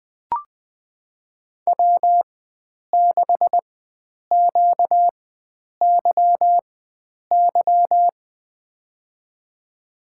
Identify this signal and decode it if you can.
Morse code, heard as a keyed tone, about 20 wpm, spaced as for 9 wpm, 700 Hz. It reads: W6QYY
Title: USA Call Signs - 4x Character Spacing - 20wpm